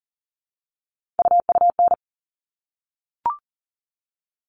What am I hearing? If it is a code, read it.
Morse code, heard as a keyed tone, about 40 wpm, 700 Hz. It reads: UUN
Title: All Cut Numbers - 3-Digit 40wpm